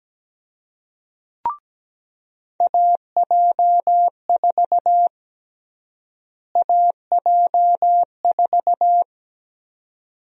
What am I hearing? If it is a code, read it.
Morse code, heard as a keyed tone, about 17 wpm, 700 Hz. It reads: AJ4 AJ4
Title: USA Call Sign Prefixes - Repeated 2x - 17wpm